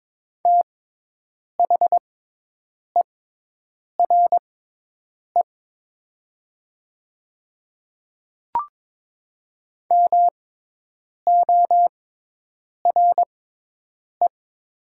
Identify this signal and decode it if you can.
Morse code, heard as a keyed tone, about 22 wpm, spaced as for 8 wpm, 700 Hz. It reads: THERE MORE
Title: 6x Character Spacing - Top 100 Words - 22wpm